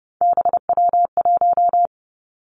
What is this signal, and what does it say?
Morse code, heard as a keyed tone, about 30 wpm, 700 Hz. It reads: BW1
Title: Three Challenging Letter-Number Combinations 30wpm